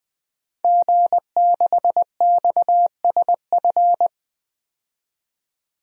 Morse code, heard as a keyed tone, about 20 wpm, 700 Hz. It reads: G6XSF
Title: DXCC-100 Call Signs - 5 Characters - 20wpm